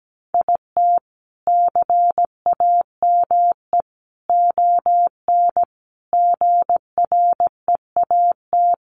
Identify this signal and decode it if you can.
Morse code, heard as a keyed tone, about 17 wpm, 700 Hz. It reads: IT CAME ON GREAT
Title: A Sound Of Thunder 17wpm